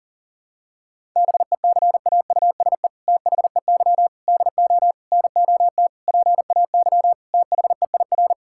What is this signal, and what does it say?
Morse code, heard as a keyed tone, about 37 wpm, 700 Hz. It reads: BECAUSE THEY DO NOT PAY THEIR
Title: Sentences from Top 1000 Words - Review Part 5 - 40wpm